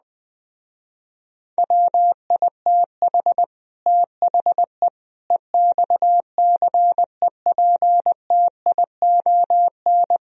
Morse code, heard as a keyed tone, about 20 wpm, 700 Hz. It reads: WITH THE EXCEPTION
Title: Speed-Racing - Sets of 4 Words 17wpm